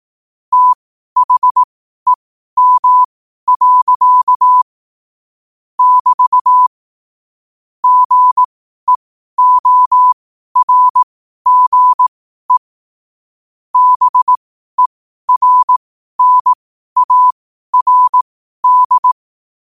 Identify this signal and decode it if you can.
Morse code, heard as a keyed tone, about 18 wpm, spaced as for 12 wpm, 1000 Hz. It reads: THEM. <BT> GEORGE BERNARD
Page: Quotes for Fri, 15 Aug 2025 in Morse Code at 12 words per minute.